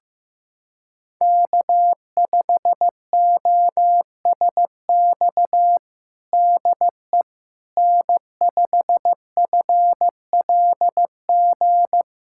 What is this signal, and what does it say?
Morse code, heard as a keyed tone, about 15 wpm, 700 Hz. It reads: K5OSX DE N5FLG
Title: USA Callsign DE Callsign - 5 Characters - 15wpm